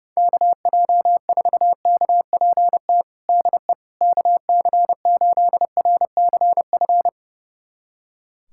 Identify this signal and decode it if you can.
Morse code, heard as a keyed tone, about 30 wpm, 700 Hz. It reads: KJ4KPT DE KC8RCF
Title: USA Callsign DE Callsign - 6 Characters - 30wpm